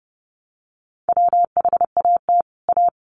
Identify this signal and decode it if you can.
Morse code, heard as a keyed tone, about 30 wpm, 700 Hz. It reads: WHAT A
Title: Sentences from Top 1100 Words 30wpm